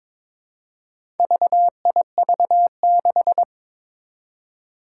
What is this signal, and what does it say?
Morse code, heard as a keyed tone, about 23 wpm, 700 Hz. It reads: VIV6
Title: Four Challenging Letter-Number Combinations 22wpm